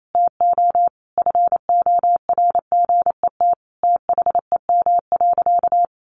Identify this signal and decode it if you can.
Morse code, heard as a keyed tone, about 28 wpm, 700 Hz. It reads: TO FORGET THEM.
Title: Sentences from Top 1200 Words 28wpm